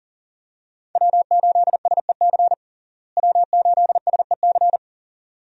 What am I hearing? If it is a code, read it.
Morse code, heard as a keyed tone, about 40 wpm, 700 Hz. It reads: W8SEC W8SEC
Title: USA Call Signs - 5 Characters - Repeated 2x - 40wpm